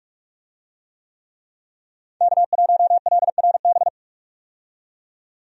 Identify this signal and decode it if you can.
Morse code, heard as a keyed tone, about 47 wpm, 700 Hz. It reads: K1LRB
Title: USA Call Signs - 5 Characters - 45wpm